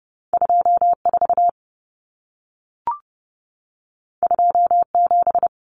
Morse code, heard as a keyed tone, about 30 wpm, 700 Hz. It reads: 24 27
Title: Numbers - 2-Digit 30wpm